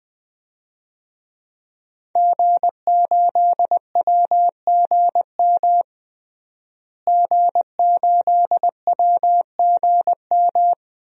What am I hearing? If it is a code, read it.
Morse code, heard as a keyed tone, about 20 wpm, 700 Hz. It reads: G8WGM G8WGM
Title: DXCC-100 Call Signs - 5 Characters - Repeated 3x - 20wpm